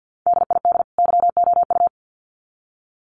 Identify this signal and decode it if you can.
Morse code, heard as a keyed tone, about 50 wpm, 700 Hz. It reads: DID YOU
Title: Sets of 2 Words 50wpm